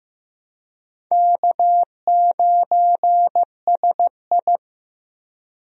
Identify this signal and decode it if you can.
Morse code, heard as a keyed tone, about 15 wpm, 700 Hz. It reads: K9SI
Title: USA Call Signs - 4 Characters - Repeated 2x - 15wpm